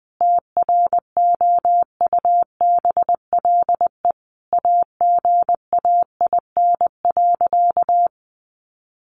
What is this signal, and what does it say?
Morse code, heard as a keyed tone, about 20 wpm, 700 Hz. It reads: TROUBLE AGAIN.
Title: Sentences from Top 1200 Words 20wpm